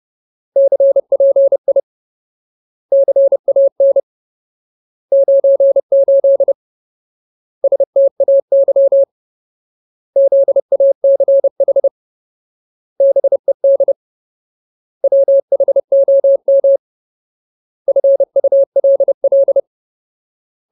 Morse code, heard as a keyed tone, about 30 wpm, 550 Hz. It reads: CPI CAN 98 STAY ZACH BED WHOM FULL
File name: Ultimate Word List / 30WPM Ultimate